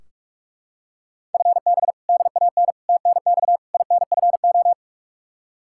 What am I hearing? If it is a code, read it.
Morse code, heard as a keyed tone, about 41 wpm, 700 Hz. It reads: FB DAN TNX INFO
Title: Practice QSO - Exchange 3 - More Info - 45wpm